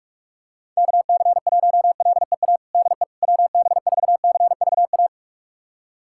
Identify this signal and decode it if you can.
Morse code, heard as a keyed tone, about 42 wpm, 690 Hz. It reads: KC1LEA DE WB4CVA
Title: USA Callsign DE Callsign 45wpm